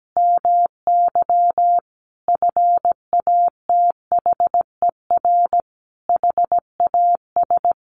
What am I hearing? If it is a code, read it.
Morse code, heard as a keyed tone, about 17 wpm, 700 Hz. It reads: MY FATHER HAS
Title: Sentences from Top 1200 Words 17wpm